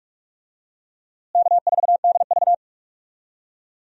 Morse code, heard as a keyed tone, about 46 wpm, 700 Hz. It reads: K4DV
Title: USA Call Signs - 4 Characters - Repeated 2x - 45wpm